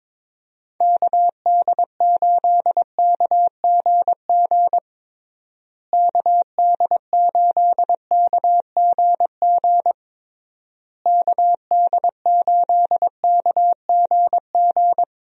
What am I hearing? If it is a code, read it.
Morse code, heard as a keyed tone, about 22 wpm, 700 Hz. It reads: KD8KGG KD8KGG KD8KGG
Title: USA Call Signs - 6 Characters - Repeated 3x - 22wpm